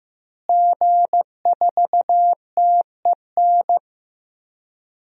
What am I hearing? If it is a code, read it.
Morse code, heard as a keyed tone, about 15 wpm, 700 Hz. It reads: G4TEN
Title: DXCC-100 Call Signs - 5 Characters - Repeated 2x - 15wpm